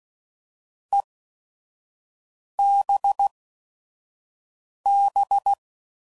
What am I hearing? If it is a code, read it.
Morse code, heard as a keyed tone, about 16 wpm, spaced as for 5 wpm, 800 Hz. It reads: EBB